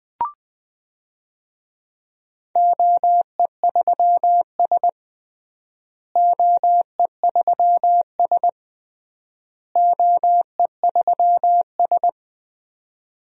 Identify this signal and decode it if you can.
Morse code, heard as a keyed tone, about 20 wpm, 700 Hz. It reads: OE3S OE3S OE3S
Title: DXCC-100 Call Signs - 4 Characters - Repeated 3x - 20wpm